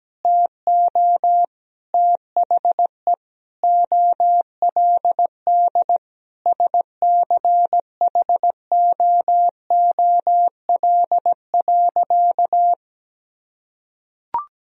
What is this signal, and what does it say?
Morse code, heard as a keyed tone, about 17 wpm, 700 Hz. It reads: TO THE OLD SCHOOL.
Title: Sentences from Top 1000 Words - Review Part 1 - 17wpm